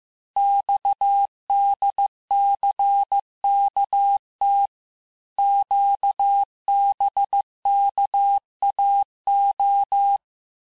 Morse code, heard as a keyed tone, about 15 wpm, 800 Hz. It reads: XDCKT QBKAO
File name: cours-du-lundi files / cours-1585-texte-chiffre-vitesse-14mpm